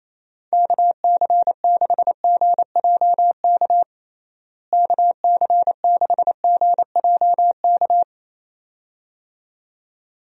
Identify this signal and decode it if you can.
Morse code, heard as a keyed tone, about 27 wpm, 700 Hz. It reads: KC6GJK KC6GJK
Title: USA Call Signs - 6 Characters - Repeated 2x - 28wpm